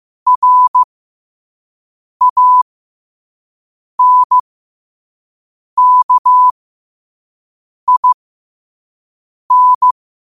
Morse code, heard as a keyed tone, about 15 wpm, spaced as for 5 wpm, 1000 Hz. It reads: RANKIN
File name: 5wpm